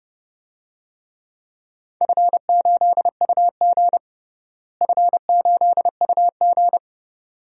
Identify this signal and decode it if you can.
Morse code, heard as a keyed tone, about 29 wpm, 700 Hz. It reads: F8UG F8UG
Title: DXCC-100 Call Signs - 4 Characters - Repeated 2x - 30wpm